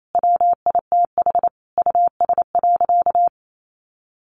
Morse code, heard as a keyed tone, about 28 wpm, 700 Hz. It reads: WITH US.
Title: Sentences from Top 1500 Words 28wpm